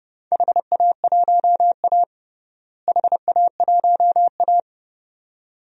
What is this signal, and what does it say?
Morse code, heard as a keyed tone, about 30 wpm, 700 Hz. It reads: HA1A HA1A
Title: DXCC-100 Call Signs - 4 Characters - Repeated 2x - 30wpm